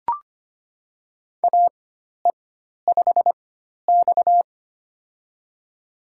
Morse code, heard as a keyed tone, about 25 wpm, spaced as for 12 wpm, 700 Hz. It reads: AE5X
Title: USA Call Signs - 4 Characters - 4x Character Spacing - 25wpm